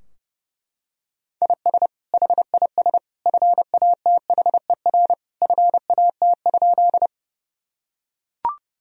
Morse code, heard as a keyed tone, about 32 wpm, 700 Hz. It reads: IS HIS FATHER FAT?
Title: Sentences from Top 1000 Words - Review Part 4 - 30wpm